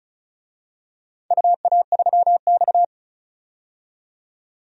Morse code, heard as a keyed tone, about 36 wpm, 700 Hz. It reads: UA3X
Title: DXCC-100 Call Signs - 4 Characters - 35wpm